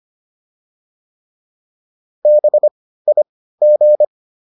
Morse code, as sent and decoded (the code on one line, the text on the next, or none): -... .. --.
BIG